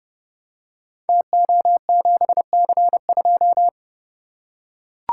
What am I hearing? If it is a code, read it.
Morse code, heard as a keyed tone, about 29 wpm, 700 Hz. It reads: TO7C2
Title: Code Groups - 5 Letter:Numbers - 30wpm